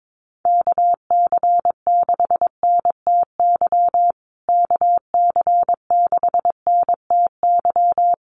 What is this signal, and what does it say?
Morse code, heard as a keyed tone, about 22 wpm, 700 Hz. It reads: KC6NTY KC6NTY
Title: USA Call Signs Repeated 2x 22wpm